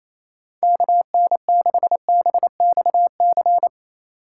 Morse code, heard as a keyed tone, about 34 wpm, 700 Hz. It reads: KN6BXC
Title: Speed Racing - USA 2x3 Call Signs 28wpm